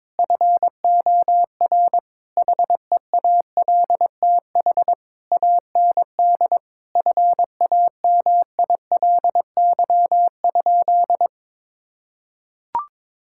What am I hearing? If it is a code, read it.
Morse code, heard as a keyed tone, about 22 wpm, 700 Hz. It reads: FOR HEALTH AND FAMILY?
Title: Sentences from Top 1600 Words 22wpm